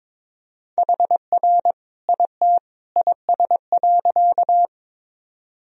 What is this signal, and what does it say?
Morse code, heard as a keyed tone, about 21 wpm, 700 Hz. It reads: HR IT IS.
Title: Sentences from Top 200 Words and Abbreviations 22wpm